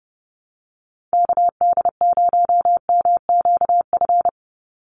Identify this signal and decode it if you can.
Morse code, heard as a keyed tone, about 30 wpm, 700 Hz. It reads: KD0MQF
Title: USA Call Signs 30wpm